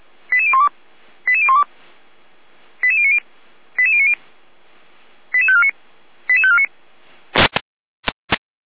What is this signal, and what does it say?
Five-tone selective call (ZVEI), heard as quick runs of five tones, tones of about 70 ms. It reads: 88012 88012 88090 88090 88549 88549